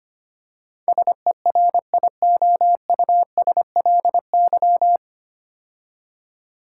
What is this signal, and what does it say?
Morse code, heard as a keyed tone, about 25 wpm, 690 Hz. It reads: SERIOUSLY
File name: practice / 25wpm / 9 Letter Words - First 3 Letters Given - 25wpm